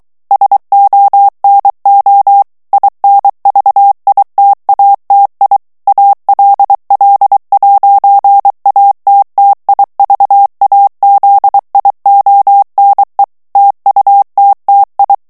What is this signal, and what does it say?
Morse code, heard as a keyed tone, about 23 wpm, 800 Hz. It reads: SONO INVITATI ALL'ATTIVAZIONE TUTTI
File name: morse mp3 / ITALIEN texte clair 1 vitesse 22 mots minute